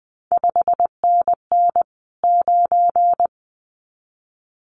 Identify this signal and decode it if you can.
Morse code, heard as a keyed tone, about 20 wpm, 700 Hz. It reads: 5NN 9
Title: CQ WW DX Contest Exchange Encore - 20wpm